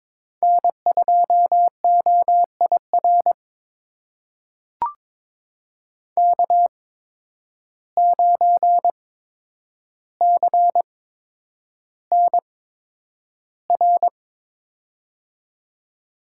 Morse code, heard as a keyed tone, about 22 wpm, spaced as for 9 wpm, 700 Hz. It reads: N2OIR K9CNR
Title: USA Call Signs - 8x Character Spacing - 22wpm